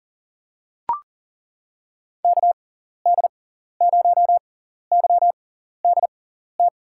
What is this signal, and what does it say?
Morse code, heard as a keyed tone, about 37 wpm, spaced as for 14 wpm, 700 Hz. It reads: KD0YDT
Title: USA Call Signs - 6 Characters - 6x Character Spacing - 40wpm